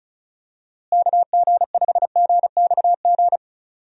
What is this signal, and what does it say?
Morse code, heard as a keyed tone, about 36 wpm, 700 Hz. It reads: KG5GXG
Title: USA Call Signs - 6 Characters - Repeated 2x - 35wpm